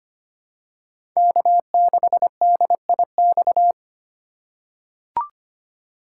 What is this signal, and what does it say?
Morse code, heard as a keyed tone, about 25 wpm, 700 Hz. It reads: K6DIX
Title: USA Call Signs - 3x Character Spacing - 25wpm